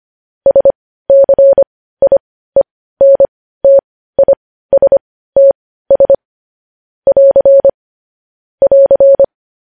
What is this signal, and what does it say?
Morse code, heard as a keyed tone, about 25 wpm, spaced as for 15 wpm, 550 Hz. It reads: SCIENTISTS <AR> <AR>
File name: podcast / News headlines-1x-25x15